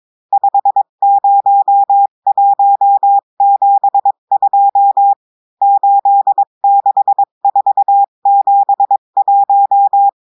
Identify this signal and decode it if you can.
Morse code, heard as a keyed tone, about 22 wpm, 800 Hz. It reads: 50172 86471
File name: cours-du-mercredi / audios / cours-853-du-2-mai-2024-etxte chiffre et texte clair-a-20-wpm